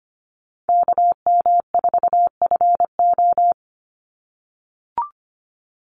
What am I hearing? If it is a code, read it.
Morse code, heard as a keyed tone, about 25 wpm, 700 Hz. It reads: KM4FO
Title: Call Signs - 25wpm